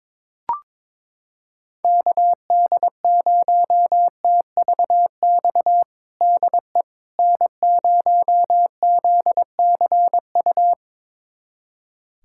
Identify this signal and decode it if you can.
Morse code, heard as a keyed tone, about 22 wpm, 700 Hz. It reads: KD0TVX DE N0ZCU
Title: USA Callsign DE Callsign 22wpm